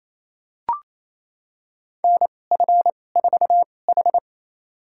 Morse code, heard as a keyed tone, about 29 wpm, spaced as for 21 wpm, 700 Hz. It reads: NF4H
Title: USA Call Signs - 4 Characters - 2x Character Spacing - 28wpm